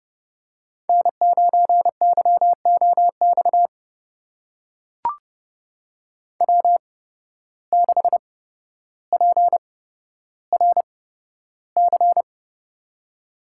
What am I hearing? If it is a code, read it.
Morse code, heard as a keyed tone, about 30 wpm, spaced as for 13 wpm, 700 Hz. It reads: N9YOX W6PRC
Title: USA Call Signs - 5 Characters - 8x Character Spacing - 30wpm